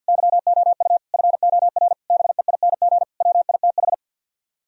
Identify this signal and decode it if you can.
Morse code, heard as a keyed tone, about 55 wpm, 700 Hz. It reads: YOU FOR BEING WITH
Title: Speed-Racing - Sets of 5 Words 50wpm